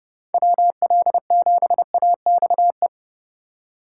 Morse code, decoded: WL7AXE

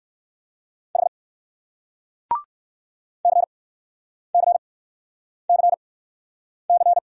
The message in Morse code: H CCCC

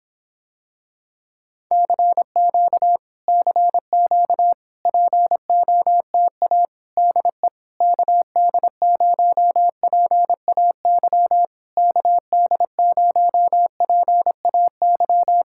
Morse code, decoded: CQ CQ POTA DE KD0PAY KD0PAY

